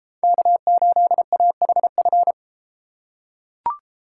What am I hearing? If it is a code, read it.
Morse code, heard as a keyed tone, about 32 wpm, 700 Hz. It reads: K8AHF